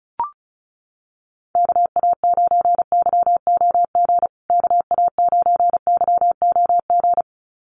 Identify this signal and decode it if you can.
Morse code, heard as a keyed tone, about 34 wpm, 700 Hz. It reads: KA9YOG KA9YOG
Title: USA Call Signs Repeated 2x 35wpm